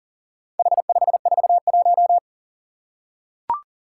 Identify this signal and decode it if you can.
Morse code, heard as a keyed tone, about 41 wpm, 690 Hz. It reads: H541
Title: Four Challenging Letter-Number Combinations 40wpm